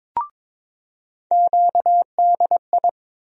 Morse code, heard as a keyed tone, about 22 wpm, 700 Hz. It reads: QDI
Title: USA Call Sign Postfixes - 3 Letters 22wpm